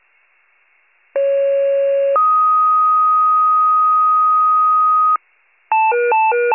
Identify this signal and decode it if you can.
Two-tone page, heard as a one-second tone, then a three-second tone, about 550 then 1250 Hz. It is heard from 1.1 to 5.2 s.